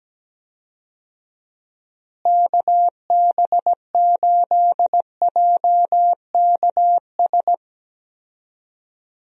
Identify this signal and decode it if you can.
Morse code, heard as a keyed tone, about 17 wpm, 700 Hz. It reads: KB8JKS
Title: USA Call Signs - 6 Characters - Repeated 2x - 17wpm